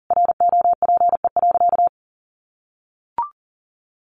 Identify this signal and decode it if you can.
Morse code, heard as a keyed tone, about 40 wpm, 700 Hz. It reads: ROPE.
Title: Easy Phrases 40wpm